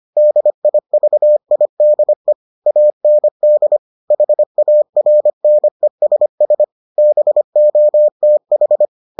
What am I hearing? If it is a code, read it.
Morse code, heard as a keyed tone, about 25 wpm, 600 Hz. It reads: DIVIDE AND HARNESS BOTH